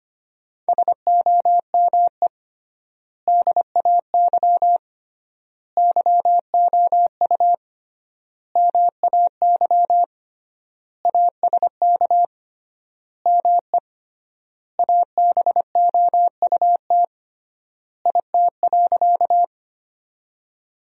Morse code, heard as a keyed tone, about 25 wpm, 700 Hz. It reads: SOME DAY YOU MAY ASK ME ABOUT IT.
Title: 3x Word-Spacing - Sentences from Top 200 Words - 25wpm